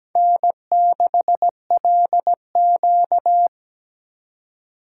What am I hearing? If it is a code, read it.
Morse code, heard as a keyed tone, about 17 wpm, 700 Hz. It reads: N6LQ
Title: USA Call Signs - 4 Characters - 4x Character Spacing - 17wpm